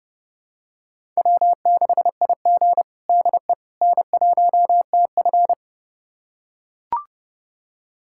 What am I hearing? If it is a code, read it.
Morse code, heard as a keyed tone, about 30 wpm, 700 Hz. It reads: W6IG DE N1TF
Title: USA Callsign DE Callsign - 4 Characters - 30wpm